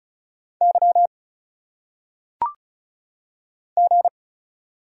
Morse code, heard as a keyed tone, about 36 wpm, 700 Hz. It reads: Y G
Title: Single Letters - Phonetic Pronunciation 35wpm